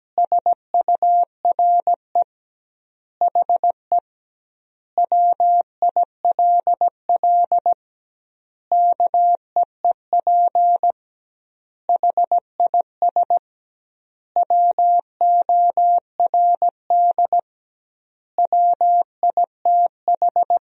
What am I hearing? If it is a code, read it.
Morse code, heard as a keyed tone, about 17 wpm, 700 Hz. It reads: SURE HE WILL KEEP HIS WORD WITH
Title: 2x Word-Spacing - Sentences from Top 300 Words - 17wpm